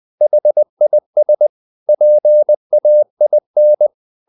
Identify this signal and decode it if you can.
Morse code, heard as a keyed tone, about 20 wpm, 600 Hz. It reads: HIS PAIN
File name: the old man and the sea part 2 20wpm